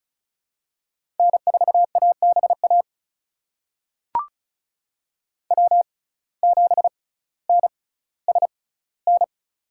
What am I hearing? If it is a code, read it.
Morse code, heard as a keyed tone, about 34 wpm, spaced as for 18 wpm, 700 Hz. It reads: N4ABA W7NSN
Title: USA Call Signs - 5 Characters - 6x Character Spacing - 35wpm